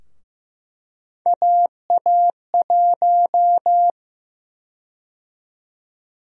Morse code, heard as a keyed tone, about 15 wpm, 700 Hz. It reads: AA1
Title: USA Call Sign Prefixes - 15wpm